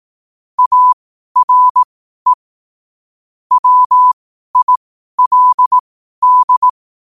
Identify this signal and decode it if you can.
Morse code, heard as a keyed tone, about 18 wpm, spaced as for 12 wpm, 1000 Hz. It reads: ARE WILD
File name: qotdfiles / 12wpm